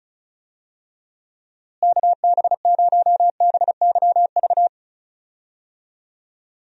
Morse code, heard as a keyed tone, about 34 wpm, 700 Hz. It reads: KB0BYV
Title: USA Call Signs - 6 Characters - Repeated 3x - 35wpm